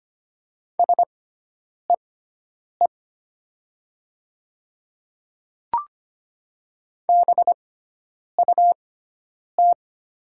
Morse code, heard as a keyed tone, about 25 wpm, spaced as for 9 wpm, 700 Hz. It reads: SEE BUT
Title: 6x Character Spacing - Top 100 Words - 25wpm